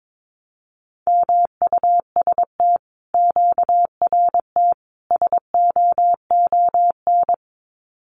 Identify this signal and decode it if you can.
Morse code, heard as a keyed tone, about 22 wpm, 700 Hz. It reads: MUST QRT SOON
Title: Common QSO Phrases Encore 22wpm